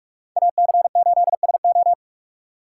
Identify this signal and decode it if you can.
Morse code, heard as a keyed tone, about 44 wpm, 700 Hz. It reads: AC8SO